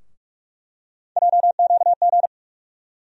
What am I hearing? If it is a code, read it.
Morse code, heard as a keyed tone, about 47 wpm, 700 Hz. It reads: JQG
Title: USA Call Sign Postfixes - 3 Letters 45wpm